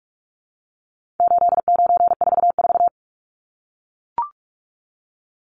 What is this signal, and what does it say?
Morse code, heard as a keyed tone, about 44 wpm, 700 Hz. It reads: 8944
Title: Numbers - 4-Digit 45wpm